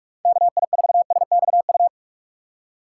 Morse code, heard as a keyed tone, about 46 wpm, 700 Hz. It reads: KI4SXU
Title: USA Call Signs - 6 Characters - 3x Character Spacing - 45wpm